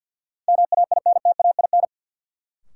Morse code, heard as a keyed tone, about 52 wpm, 700 Hz. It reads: MAINTAIN
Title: Sentences from Top 1000 Words - Review Part 5 - New Word Drill - 50wpm